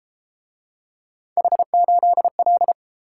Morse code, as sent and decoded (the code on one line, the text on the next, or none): .... ---.. .-..
H8L